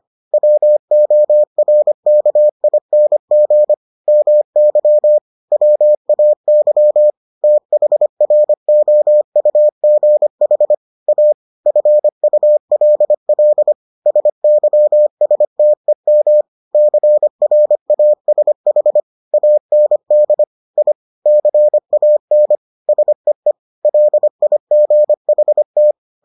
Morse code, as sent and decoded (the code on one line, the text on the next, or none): .-- --- .-. -.- .. -. --. / -- -.-- / .-- .- -.-- / - .... .-. --- ..- --. .... / .- / ..-. ..- .-.. .-.. / ... -.-- ... - . -- / -.-. .-. .- ... .... / .- -. -.. / .. / -.-. .- -. / ... . / .-.. .. --. .... -
WORKING MY WAY THROUGH A FULL SYSTEM CRASH AND I CAN SEE LIGHT